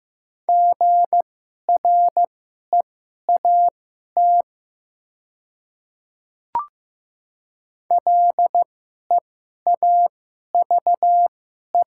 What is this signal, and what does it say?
Morse code, heard as a keyed tone, about 15 wpm, spaced as for 11 wpm, 700 Hz. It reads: GREAT LEAVE
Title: 2x Character Spacing - Top 500 Words - 15wpm